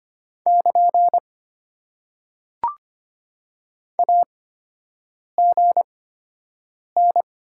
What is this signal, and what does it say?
Morse code, heard as a keyed tone, about 25 wpm, spaced as for 7 wpm, 700 Hz. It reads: <KN> AGN
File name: Top 90 QSO Elements - 8x Character Spacing - 25wpm